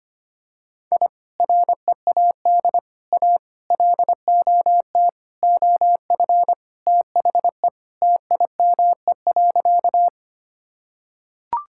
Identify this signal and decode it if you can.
Morse code, heard as a keyed tone, about 25 wpm, 700 Hz. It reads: I READ A LOT OF THE TIME.
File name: practice / 25wpm / Sentences with 4-Word Phrases - 25wpm